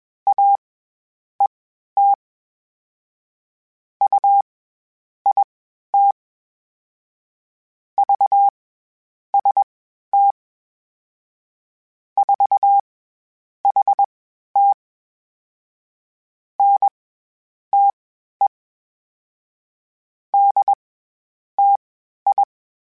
Morse code, heard as a keyed tone, about 20 wpm, spaced as for 10 wpm, 800 Hz. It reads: AET UIT VST 4HT NTE DTI
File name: CW Comparisons / CW Comparisons-mp3s / CW Comparisons Broken Characters @20